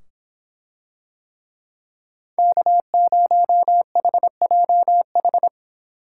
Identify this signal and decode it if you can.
Morse code, heard as a keyed tone, about 25 wpm, 700 Hz. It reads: K0HJH